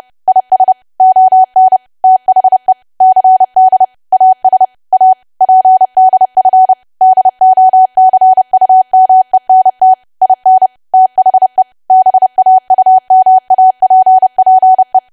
Morse code, heard as a keyed tone, about 30 wpm, 750 Hz. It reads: IS ON THE CD AS A PDF DOCUMENT IN THE BAUMAPPE
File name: cwa / W1AW-Level3 / 30WPM